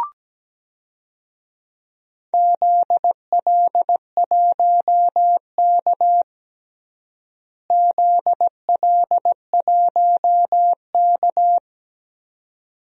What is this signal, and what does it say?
Morse code, heard as a keyed tone, about 17 wpm, 700 Hz. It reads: ZL1K ZL1K
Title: DXCC-100 Call Signs - 4 Characters - Repeated 2x - 17wpm